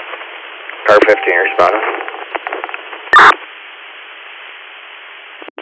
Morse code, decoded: none

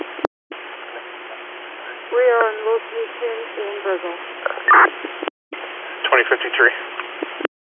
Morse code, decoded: none

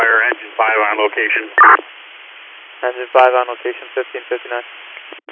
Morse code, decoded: none

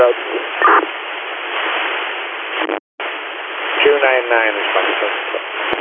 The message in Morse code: none

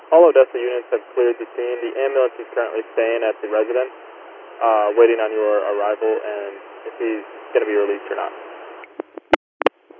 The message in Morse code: none

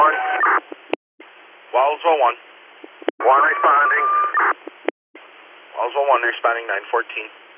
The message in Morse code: none

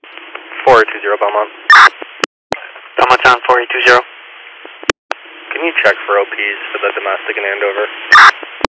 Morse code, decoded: none

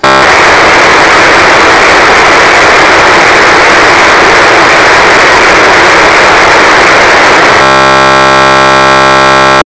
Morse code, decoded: none

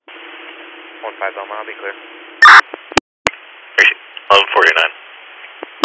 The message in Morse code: none

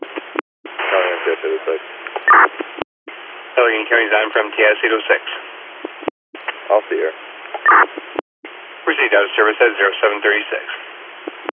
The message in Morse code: none

